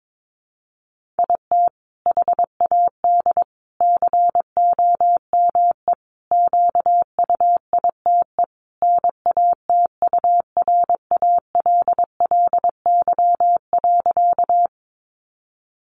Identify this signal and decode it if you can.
Morse code, heard as a keyed tone, about 22 wpm, 700 Hz. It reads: IT HAD COME QUITE NATURALLY.
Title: Sentences from Top 1300 Words 22wpm